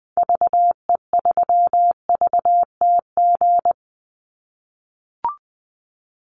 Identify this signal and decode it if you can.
Morse code, heard as a keyed tone, about 20 wpm, 700 Hz. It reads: VE3VTG
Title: Call Signs - 20wpm